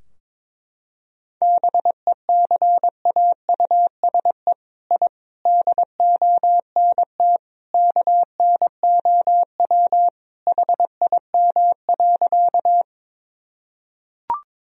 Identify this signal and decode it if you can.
Morse code, heard as a keyed tone, about 22 wpm, 700 Hz. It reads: BECAUSE I DONT KNOW HIM.
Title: Sentences from Top 1000 Words - Review Part 3 - 22wpm